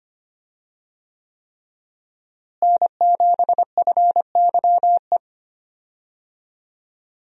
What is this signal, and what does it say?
Morse code, heard as a keyed tone, about 25 wpm, 700 Hz. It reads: N7FYE
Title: USA Call Signs - 5 Characters - Repeated 3x - 25wpm